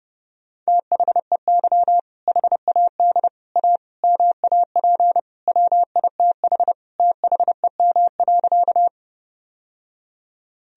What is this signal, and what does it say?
Morse code, heard as a keyed tone, about 32 wpm, 700 Hz. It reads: THEY HAD A MAP WITH THEM.
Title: Sentences from Top 1000 Words - Review Part 2 - 30wpm